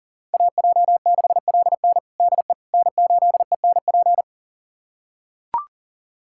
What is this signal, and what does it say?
Morse code, heard as a keyed tone, about 40 wpm, 700 Hz. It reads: AJ6LN DE N8ENP
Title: USA Callsign DE Callsign - 5 Characters - 40wpm